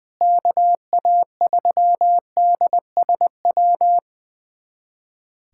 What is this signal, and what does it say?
Morse code, heard as a keyed tone, about 20 wpm, 700 Hz. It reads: KA3DSW
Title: USA Callsign DE Callsign - 6 Characters - 20wpm